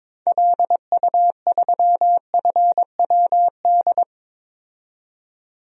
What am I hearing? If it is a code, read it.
Morse code, heard as a keyed tone, about 21 wpm, 700 Hz. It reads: LU3FWD
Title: DXCC-100 Call Signs - 6 Characters - 22wpm